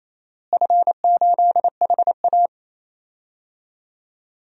Morse code, heard as a keyed tone, about 28 wpm, 700 Hz. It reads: F8HA
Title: DXCC-100 Call Signs - 4 Characters - Repeated 2x - 28wpm